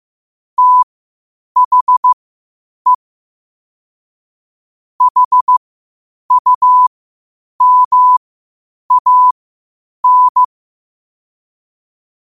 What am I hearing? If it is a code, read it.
Morse code, heard as a keyed tone, about 15 wpm, spaced as for 8 wpm, 1000 Hz. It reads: THE HUMAN